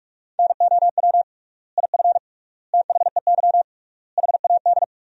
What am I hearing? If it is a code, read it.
Morse code, heard as a keyed tone, about 46 wpm, 700 Hz. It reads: NOW IF THEY HAD
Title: 3x Word-Spacing - Sentences from Top 500 Words - 45wpm